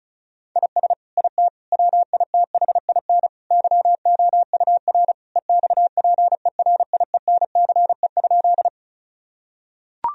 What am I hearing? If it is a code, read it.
Morse code, heard as a keyed tone, about 36 wpm, 700 Hz. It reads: IS IT WITHIN YOUR EXPERIENCE?
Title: Sentences from Top 1000 Words - Review Part 4 - 35wpm